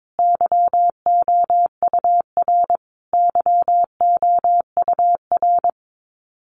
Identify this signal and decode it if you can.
Morse code, heard as a keyed tone, about 22 wpm, 700 Hz. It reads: YOUR YOUR
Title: Top 200 Words Repeated 22wpm